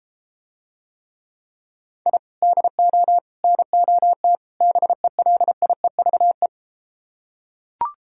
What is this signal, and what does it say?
Morse code, heard as a keyed tone, about 33 wpm, 700 Hz. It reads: I DO NOT BELIEVE